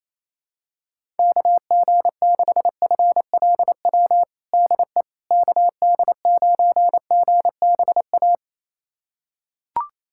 Morse code, heard as a keyed tone, about 28 wpm, 700 Hz. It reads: KG6FLW DE KD9GBA
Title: USA Callsign DE Callsign 28wpm